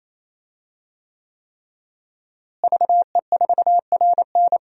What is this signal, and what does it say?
Morse code, heard as a keyed tone, about 28 wpm, 700 Hz. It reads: VE4RN
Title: DXCC-100 Call Signs - 5 Characters - Repeated 3x - 28wpm